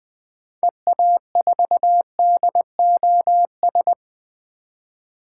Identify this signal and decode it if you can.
Morse code, heard as a keyed tone, about 20 wpm, 700 Hz. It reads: EA4DOS
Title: 2023 SOTA Activator Call Signs - 20wpm